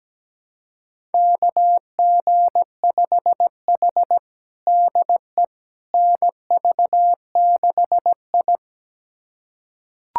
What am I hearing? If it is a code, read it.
Morse code, heard as a keyed tone, about 17 wpm, 700 Hz. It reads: KG5H DE NV6I
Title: USA Callsign DE Callsign - 4 Characters - 17wpm